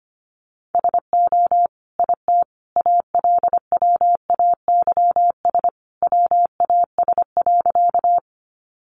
Morse code, heard as a keyed tone, about 25 wpm, 700 Hz. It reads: SO IT ALWAYS WAS.
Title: Sentences from Top 300 Words 25wpm